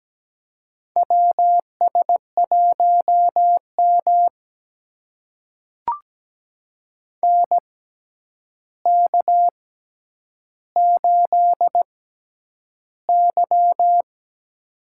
Morse code, heard as a keyed tone, about 17 wpm, spaced as for 9 wpm, 700 Hz. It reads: WS1M NK8Y
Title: USA Call Signs - 4 Characters - 6x Character Spacing - 17wpm